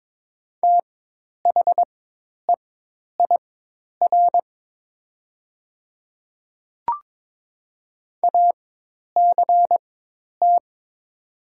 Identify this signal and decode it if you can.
Morse code, heard as a keyed tone, about 22 wpm, spaced as for 10 wpm, 700 Hz. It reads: THEIR ACT
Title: 4x Character Spacing - Top 200 Words - 22wpm